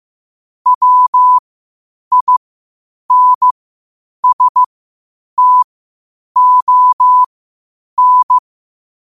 Morse code, heard as a keyed tone, about 15 wpm, spaced as for 8 wpm, 1000 Hz. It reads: WINSTON
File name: qotdfiles / 8wpm